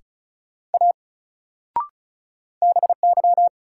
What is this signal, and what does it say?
Morse code, heard as a keyed tone, about 36 wpm, 700 Hz. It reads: A BY
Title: Sentences from Top 100 Words - New Word Drill - 35wpm